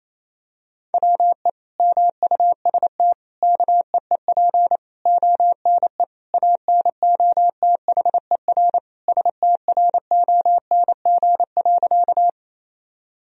Morse code, heard as a keyed tone, about 28 wpm, 700 Hz. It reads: WE MUST KEEP ONE ANOTHER STRONG.
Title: Sentences from Top 1000 Words - Review Part 3 - 28wpm